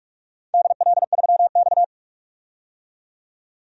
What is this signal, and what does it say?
Morse code, heard as a keyed tone, about 44 wpm, 700 Hz. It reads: DL3X
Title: DXCC-100 Call Signs - 4 Characters - 45wpm